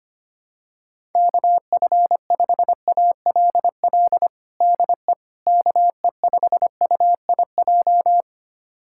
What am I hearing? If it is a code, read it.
Morse code, heard as a keyed tone, about 25 wpm, 700 Hz. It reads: KF5ALL DE KE5UIJ
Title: USA Callsign DE Callsign 25wpm